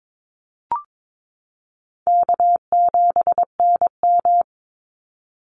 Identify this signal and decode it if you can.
Morse code, heard as a keyed tone, about 22 wpm, 700 Hz. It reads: K7NM
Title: Easy Call Signs - 22wpm